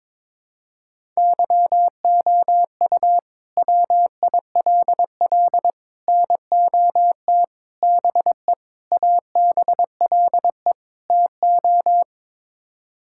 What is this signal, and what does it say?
Morse code, heard as a keyed tone, about 22 wpm, 700 Hz. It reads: YOU WILL NOT BE ABLE TO
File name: practice / 22wpm / Sets of 6 Words - 22wpm